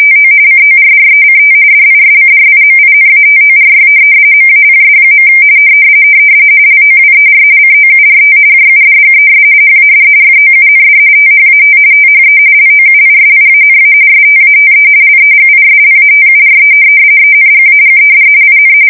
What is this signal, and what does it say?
RTTY, about 45.45 baud. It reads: TO CORRECT DISCRIMINATORY MAPS. LOWER COURT JUDGES HAD FOUND THAT THE LOUISIANA STATE LEGISLATURE ENGAGED IN
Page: 170 Shift 2125/2295